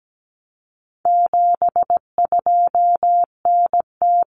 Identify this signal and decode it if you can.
Morse code, heard as a keyed tone, about 17 wpm, 700 Hz. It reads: 72NT
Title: Common Cut Numbers - 4-Digit 17wpm